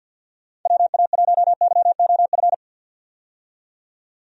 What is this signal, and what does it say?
Morse code, heard as a keyed tone, about 53 wpm, 700 Hz. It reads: WA1YOF
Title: USA Call Signs - 6 Characters - 50wpm